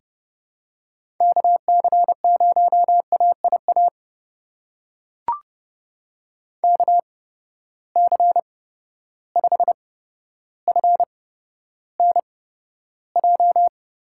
Morse code, heard as a keyed tone, about 30 wpm, spaced as for 13 wpm, 700 Hz. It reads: KC0AIA KC5FNJ
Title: USA Call Signs - 6 Characters - 8x Character Spacing - 30wpm